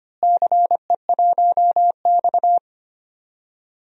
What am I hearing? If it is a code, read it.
Morse code, heard as a keyed tone, about 25 wpm, 700 Hz. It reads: CE1X
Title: DXCC-100 Call Signs - 4 Characters - Repeated 2x - 25wpm